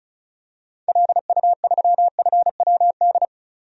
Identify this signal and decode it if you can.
Morse code, heard as a keyed tone, about 36 wpm, 700 Hz. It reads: LU3FWD
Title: DXCC-100 Call Signs - 6 Characters - 35wpm